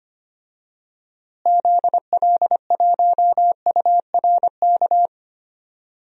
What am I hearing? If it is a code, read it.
Morse code, heard as a keyed tone, about 25 wpm, 700 Hz. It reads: ZL1URK
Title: DXCC-100 Call Signs - 6 Characters - Repeated 3x - 25wpm